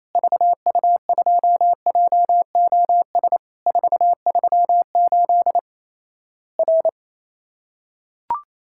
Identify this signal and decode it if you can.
Morse code, heard as a keyed tone, about 27 wpm, 700 Hz. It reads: VU2JOS 438 R
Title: Worked All Europe DX Contest - QTC Exchange - 28wpm